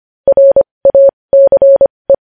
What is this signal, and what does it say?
Morse code, heard as a keyed tone, about 25 wpm, spaced as for 20 wpm, 550 Hz. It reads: RACE
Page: news-headlines-in-morse-code-at-20-wpm